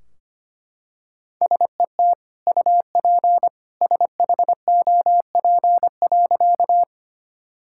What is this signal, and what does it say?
Morse code, heard as a keyed tone, about 25 wpm, 700 Hz. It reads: SET UP SHOP.
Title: Sentences from Top 1600 Words 25wpm